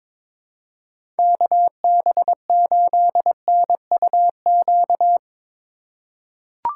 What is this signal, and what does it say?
Morse code, heard as a keyed tone, about 21 wpm, 700 Hz. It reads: KB8NUQ
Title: USA Call Signs - 6 Characters - 3x Character Spacing - 22wpm